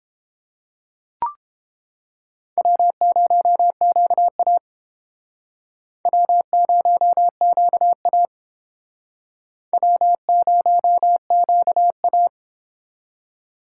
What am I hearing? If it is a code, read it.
Morse code, heard as a keyed tone, about 32 wpm, 700 Hz. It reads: W0QA W0QA W0QA